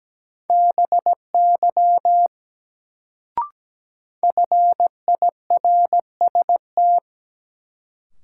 Morse code, heard as a keyed tone, about 17 wpm, 700 Hz. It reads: BY FIRST
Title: Sets of 2 Words - New Word Drill - 17wpm